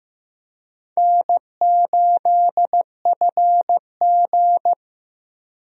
Morse code, heard as a keyed tone, about 15 wpm, 700 Hz. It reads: N8FG
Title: USA Call Signs - 4 Characters - Repeated 2x - 15wpm